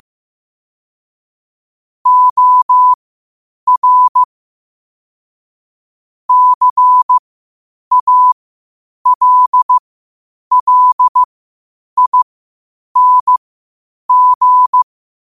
Morse code, heard as a keyed tone, about 15 wpm, spaced as for 8 wpm, 1000 Hz. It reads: OR CALLING
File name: qotdfiles / 8wpm